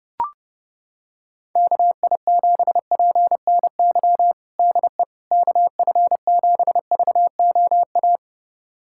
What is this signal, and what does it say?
Morse code, heard as a keyed tone, about 30 wpm, 700 Hz. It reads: KI7PNY DE KF7VOA
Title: USA Callsign DE Callsign - 6 Characters - 30wpm